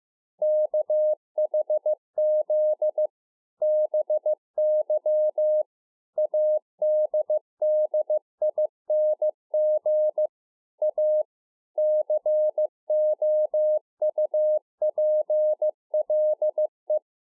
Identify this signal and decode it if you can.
Morse code, heard as a keyed tone, about 15 wpm, 610 Hz. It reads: KHZ BY ADDING A COUPLE
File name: 2011-04-mp3 / A SMALL TRANCEIVER 2 15WPM